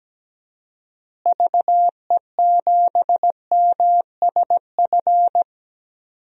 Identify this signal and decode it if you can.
Morse code, heard as a keyed tone, about 17 wpm, 700 Hz. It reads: VE7MSF
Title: DXCC-100 Call Signs - 6 Characters - Repeated 2x - 17wpm